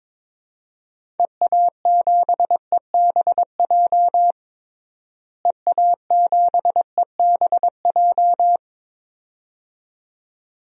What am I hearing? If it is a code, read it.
Morse code, heard as a keyed tone, about 22 wpm, 700 Hz. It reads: EA7EBJ EA7EBJ
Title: DXCC-100 Call Signs - 6 Characters - Repeated 2x - 22wpm